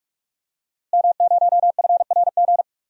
Morse code, heard as a keyed tone, about 46 wpm, 700 Hz. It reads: M0FRG
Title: DXCC-100 Call Signs - 5 Characters - Repeated 2x - 45wpm